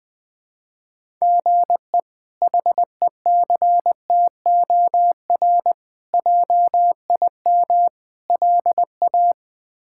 Morse code, heard as a keyed tone, about 20 wpm, 700 Hz. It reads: GE HECTOR JIM LA
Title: K1USN SST QSO Practice - 20wpm